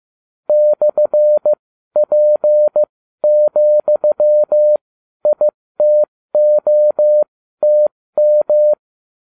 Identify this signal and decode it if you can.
Morse code, heard as a keyed tone, about 15 wpm, spaced as for 12 wpm, 600 Hz. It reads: /P,ITOTM